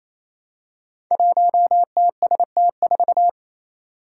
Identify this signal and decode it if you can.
Morse code, heard as a keyed tone, about 28 wpm, 700 Hz. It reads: 1TST4
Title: Five Challenging Letter-Number Combinations 28wpm